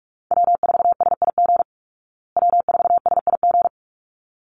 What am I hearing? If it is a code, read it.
Morse code, heard as a keyed tone, about 44 wpm, 700 Hz. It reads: W4SIG W4SIG
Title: Call Signs Repeated 3x - 45wpm